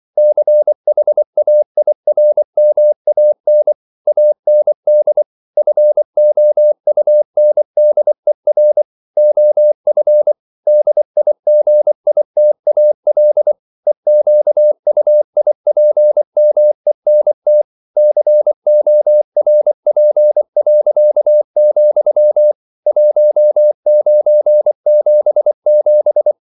Morse code, heard as a keyed tone, about 24 wpm, 600 Hz. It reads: CHAIRMAN AND FOUNDER OF DIGITAL EQUIPMENT CORP., 1977
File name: CW / Practice / EnglishQuotations0424